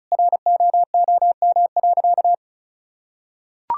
Morse code, heard as a keyed tone, about 34 wpm, 700 Hz. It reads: ROOM.
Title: Sentences from Top 1000 Words - Review Part 6 - 35wpm